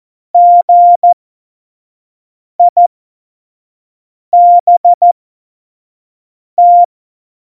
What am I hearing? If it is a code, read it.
Morse code, heard as a keyed tone, about 14 wpm, spaced as for 5 wpm, 700 Hz. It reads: GIBT
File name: Morsekurs / Witze / Witz 11 Juli 17, Tempo 25 F70